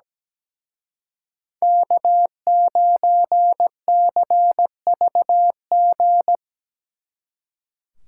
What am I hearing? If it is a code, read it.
Morse code, heard as a keyed tone, about 20 wpm, 700 Hz. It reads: K9CVG
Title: Speed Racing - USA Call Signs 17wpm